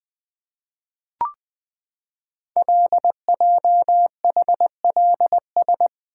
Morse code, heard as a keyed tone, about 20 wpm, 700 Hz. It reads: LJHLS
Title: Code Groups - 5 Letters - 20wpm